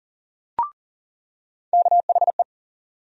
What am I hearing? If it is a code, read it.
Morse code, heard as a keyed tone, about 40 wpm, 700 Hz. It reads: KHE
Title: USA Call Sign Postfixes - 3 Letters 40wpm